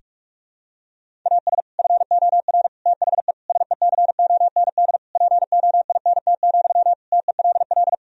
Morse code, heard as a keyed tone, about 41 wpm, 700 Hz. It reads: AS FOR THE SECOND POINT, TELL
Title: Sentences from Top 1000 Words - Review Part 1 - 45wpm